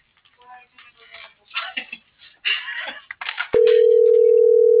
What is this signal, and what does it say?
Ringback tone (North America), 440 + 480 Hz.